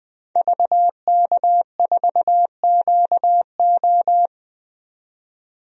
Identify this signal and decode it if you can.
Morse code, heard as a keyed tone, about 20 wpm, 700 Hz. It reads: VK4QO
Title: DXCC-100 Call Signs - 5 Characters - 20wpm